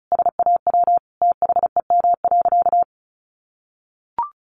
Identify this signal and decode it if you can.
Morse code, heard as a keyed tone, about 34 wpm, 700 Hz. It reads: SAW THEM.
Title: Sentences from Top 1300 Words 35wpm